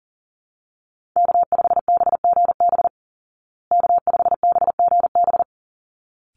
Morse code, heard as a keyed tone, about 40 wpm, 700 Hz. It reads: K5BGB K5BGB
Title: Call Signs Repeated 3x - 40wpm